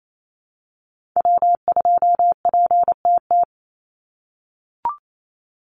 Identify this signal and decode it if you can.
Morse code, heard as a keyed tone, about 28 wpm, 700 Hz. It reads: W2PTT
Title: Call Signs - 28wpm